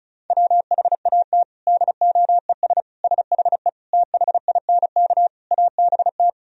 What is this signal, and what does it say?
Morse code, heard as a keyed tone, about 35 wpm, 700 Hz. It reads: WHAT DOES SHE THINK ABT
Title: Sentences from Top 200 Words and Abbreviations 35wpm